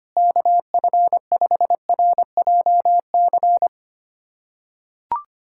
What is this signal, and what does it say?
Morse code, heard as a keyed tone, about 25 wpm, 700 Hz. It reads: KF5RJC
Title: USA Callsign DE Callsign 25wpm